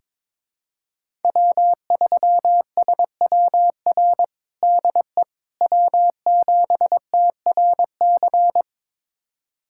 Morse code, heard as a keyed tone, about 22 wpm, 700 Hz. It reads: W3SWR DE W7TRC
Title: USA Callsign DE Callsign - 5 Characters - 22wpm